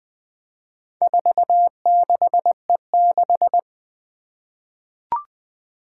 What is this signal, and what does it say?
Morse code, heard as a keyed tone, about 20 wpm, 700 Hz. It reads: 46E6
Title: Four Challenging Letter-Number Combinations 20wpm